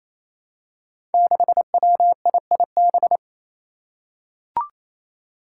Morse code, heard as a keyed tone, about 28 wpm, 700 Hz. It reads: 6WIIB
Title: Five Challenging Letter-Number Combinations 28wpm